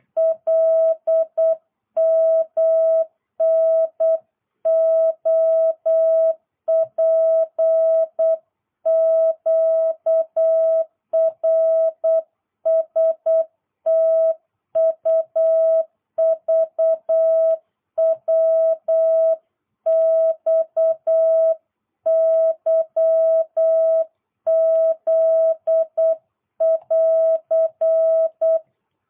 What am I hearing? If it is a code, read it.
Morse code, heard as a keyed tone, about 8 wpm, 650 Hz. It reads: LMNOPQRSTUVWXYZ<AR>